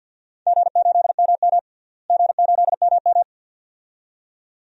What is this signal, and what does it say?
Morse code, heard as a keyed tone, about 51 wpm, 700 Hz. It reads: G8MM G8MM
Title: DXCC-100 Call Signs - 4 Characters - Repeated 2x - 50wpm